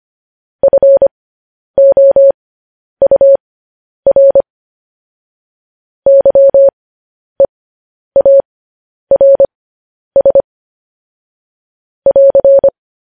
Morse code, heard as a keyed tone, about 25 wpm, spaced as for 10 wpm, 550 Hz. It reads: FOUR YEARS <AR>
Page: news-headlines-in-morse-code-at-10-wpm